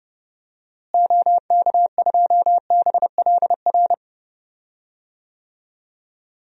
Morse code, heard as a keyed tone, about 31 wpm, 700 Hz. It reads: OK2BLR
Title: DXCC-100 Call Signs - 6 Characters - Repeated 3x - 30wpm